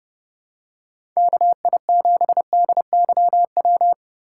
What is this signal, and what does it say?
Morse code, heard as a keyed tone, about 30 wpm, 700 Hz. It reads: KI7DYW
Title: USA Call Signs - 3x Character Spacing - 30wpm